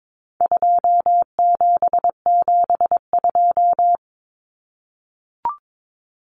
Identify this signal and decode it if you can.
Morse code, heard as a keyed tone, about 22 wpm, 700 Hz. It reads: 2772
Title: Common Cut Numbers - 4-Digit 22wpm